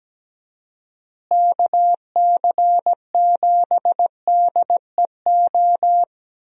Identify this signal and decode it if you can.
Morse code, heard as a keyed tone, about 17 wpm, 700 Hz. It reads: KC7DEO
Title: USA Call Signs - 6 Characters - Repeated 2x - 17wpm